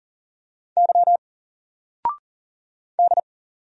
Morse code, heard as a keyed tone, about 40 wpm, 700 Hz. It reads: Y D
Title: Single Letter-Number Alt 40wpm